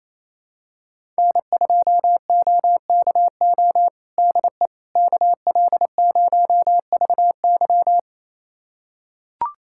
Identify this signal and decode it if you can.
Morse code, heard as a keyed tone, about 28 wpm, 700 Hz. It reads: N2OKO DE KL0VY
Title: USA Callsign DE Callsign 28wpm